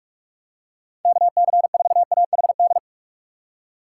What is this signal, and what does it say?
Morse code, heard as a keyed tone, about 48 wpm, 700 Hz. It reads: KC4AHD
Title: USA Call Signs - 6x Character Spacing - 45wpm